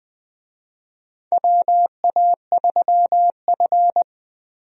Morse code, heard as a keyed tone, about 21 wpm, 700 Hz. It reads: WA3F
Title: USA Call Signs - 4 Characters - Repeated 3x - 20wpm